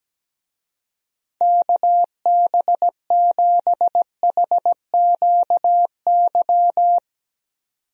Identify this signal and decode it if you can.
Morse code, heard as a keyed tone, about 17 wpm, 700 Hz. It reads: KB7HQY
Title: USA Call Signs - 6 Characters - 17wpm